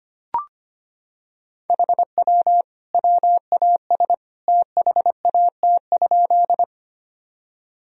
Morse code, heard as a keyed tone, about 25 wpm, 700 Hz. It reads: HW WAS THAT?
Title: Sentences from Top 100 Words and Abbreviations 25wpm